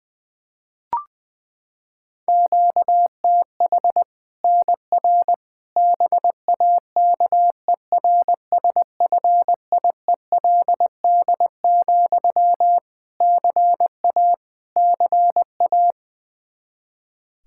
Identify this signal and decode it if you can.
Morse code, heard as a keyed tone, about 21 wpm, 700 Hz. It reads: QTH NR BAKERSFIELD, CA CA
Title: Practice QSO - Exchange 1 - Beginning - 20wpm